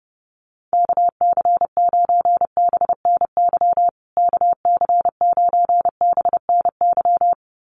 Morse code, heard as a keyed tone, about 30 wpm, 700 Hz. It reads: KC9BNY KC9BNY
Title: USA Call Signs Repeated 2x 30wpm